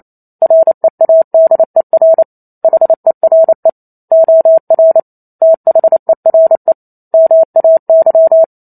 Morse code, heard as a keyed tone, about 28 wpm, 650 Hz. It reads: READER HERE OR THERE MAY
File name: Chapter 2 - 28 WPM